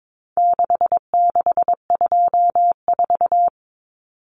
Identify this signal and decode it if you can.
Morse code, heard as a keyed tone, about 22 wpm, 700 Hz. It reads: 6624
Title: Numbers - 4-Digit 22wpm